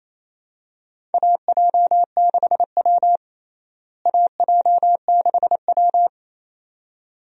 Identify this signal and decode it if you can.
Morse code, heard as a keyed tone, about 28 wpm, 700 Hz. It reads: AJ6W AJ6W
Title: USA Call Signs - 4 Characters - Repeated 2x - 28wpm